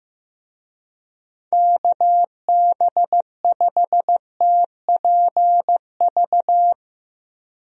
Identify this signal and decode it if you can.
Morse code, heard as a keyed tone, about 15 wpm, 700 Hz. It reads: KB5TPV
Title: USA Call Signs - 6 Characters - Repeated 3x - 15wpm